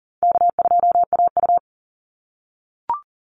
Morse code, heard as a keyed tone, about 40 wpm, 700 Hz. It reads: K2AU
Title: Call Signs - 40wpm